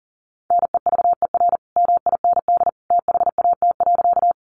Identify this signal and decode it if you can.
Morse code, heard as a keyed tone, about 40 wpm, 700 Hz. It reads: NEVER MIND THAT.
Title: Sentences from Top 500 Words 40wpm